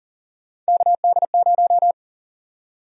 Morse code, heard as a keyed tone, about 40 wpm, 700 Hz. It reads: KD0
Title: USA Call Sign Prefixes - Repeated 2x - 40wpm